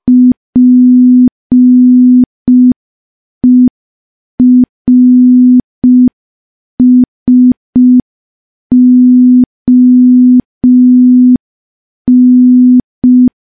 Morse code, heard as a keyed tone, about 5 wpm, 250 Hz. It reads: PERSON
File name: morse / morse-training-05WPM